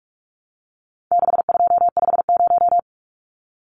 Morse code, heard as a keyed tone, about 44 wpm, 700 Hz. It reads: 6250
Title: Numbers - 4-Digit 45wpm